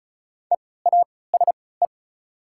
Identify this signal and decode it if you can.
Morse code, heard as a keyed tone, about 36 wpm, spaced as for 20 wpm, 700 Hz. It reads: EASE
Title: 3x Character Spacing - Top 500 Words - 35wpm